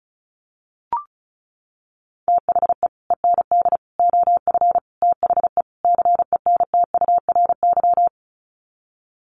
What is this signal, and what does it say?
Morse code, heard as a keyed tone, about 34 wpm, 700 Hz. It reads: THE END OF THE CENTURY
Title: Sets of 5 Words Encore2 35wpm